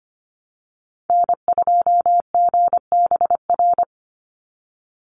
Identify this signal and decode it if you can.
Morse code, heard as a keyed tone, about 25 wpm, 700 Hz. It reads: N2GBR
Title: NA SOTA Activator Callsigns 25wpm